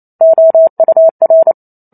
Morse code, heard as a keyed tone, about 28 wpm, 650 Hz. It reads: OUR